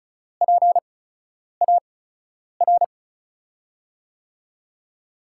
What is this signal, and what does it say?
Morse code, heard as a keyed tone, about 36 wpm, spaced as for 10 wpm, 700 Hz. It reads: PAR
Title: USA Call Signs - 5 Characters - 8x Character Spacing - 35wpm